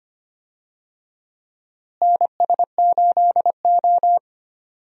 Morse code, heard as a keyed tone, about 25 wpm, 700 Hz. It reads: NS8O
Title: USA Call Signs - 4 Characters - Repeated 2x - 25wpm